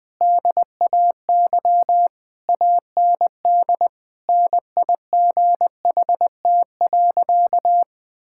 Morse code, heard as a keyed tone, about 20 wpm, 700 Hz. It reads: DAY AND NIGHT.
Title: Sentences with 8-Word Phrases - 20wpm